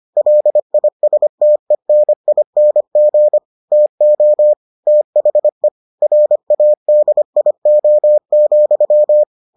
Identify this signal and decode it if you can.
Morse code, heard as a keyed tone, about 25 wpm, 600 Hz. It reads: LISTENING TO THE RADIO,